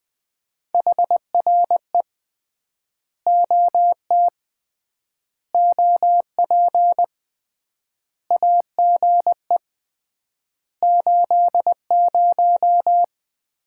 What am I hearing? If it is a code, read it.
Morse code, heard as a keyed tone, about 20 wpm, 700 Hz. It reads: HRE OT OP AGE 80
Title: Common QSO Phrases - 3x Word-Spacing - 20wpm